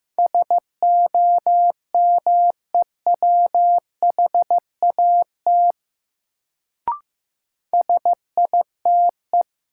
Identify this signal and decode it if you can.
Morse code, heard as a keyed tone, about 15 wpm, 700 Hz. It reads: SOMEWHAT SITE
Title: Sentences from Top 1000 Words - Review Part 1 - New Word Drill - 15wpm